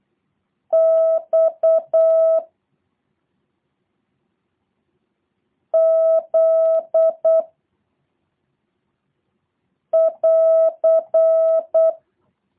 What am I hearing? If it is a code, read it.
Morse code, heard as a keyed tone, about 8 wpm, 650 Hz. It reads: X Z <AR>